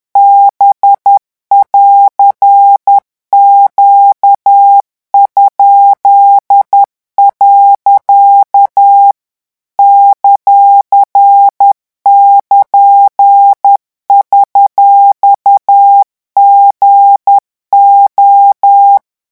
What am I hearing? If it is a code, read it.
Morse code, heard as a keyed tone, about 10 wpm, 800 Hz. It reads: B<AR>Q?. ;<KN>$GO
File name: morse mp3 / revision19-vitesse 10 mots